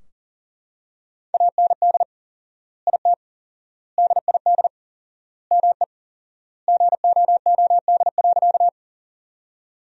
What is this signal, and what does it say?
Morse code, heard as a keyed tone, about 40 wpm, 700 Hz. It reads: AND IT DID ME GOOD.
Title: 4x Word-Spacing - Sentences from Top 200 Words - 40wpm